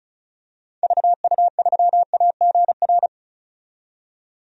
Morse code, heard as a keyed tone, about 34 wpm, 700 Hz. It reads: VU3AGR
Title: DXCC-100 Call Signs - 6 Characters - 35wpm